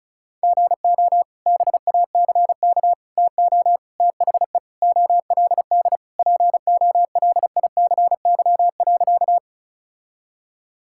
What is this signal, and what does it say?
Morse code, heard as a keyed tone, about 34 wpm, 700 Hz. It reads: GO BACK TO THE OLD POLICY.
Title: Sentences from Top 1000 Words - Review Part 5 - 35wpm